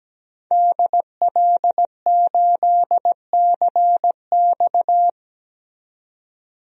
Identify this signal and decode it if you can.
Morse code, heard as a keyed tone, about 17 wpm, 700 Hz. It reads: DL8CX
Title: DXCC-100 Call Signs - 5 Characters - 17wpm